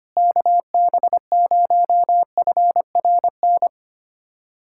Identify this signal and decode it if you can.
Morse code, heard as a keyed tone, about 25 wpm, 700 Hz. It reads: KB0FRN
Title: USA Callsign DE Callsign - 6 Characters - 25wpm